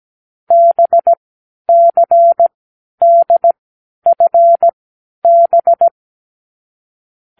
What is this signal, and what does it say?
Morse code, heard as a keyed tone, about 17 wpm, spaced as for 10 wpm, 680 Hz. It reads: BCDFB
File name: k7qo / Lesson9(A-F)